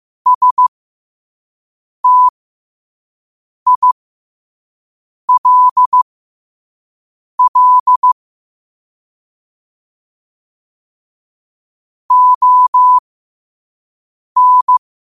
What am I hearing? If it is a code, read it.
Morse code, heard as a keyed tone, about 15 wpm, spaced as for 5 wpm, 1000 Hz. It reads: STILL ON
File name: qotdfiles / 5wpm